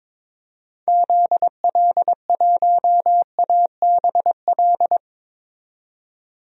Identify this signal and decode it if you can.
Morse code, heard as a keyed tone, about 22 wpm, 700 Hz. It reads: ZL1ABL
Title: DXCC-100 Call Signs - 6 Characters - 22wpm